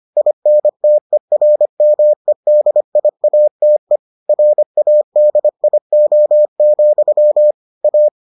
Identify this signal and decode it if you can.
Morse code, heard as a keyed tone, about 25 wpm, 600 Hz. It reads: INTERMEDIATE RADIO, A